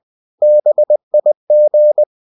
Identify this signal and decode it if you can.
Morse code, heard as a keyed tone, about 20 wpm, 600 Hz. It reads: BIG
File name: cw-books / winnie-the-pooh / winnie-20wpm-page-0043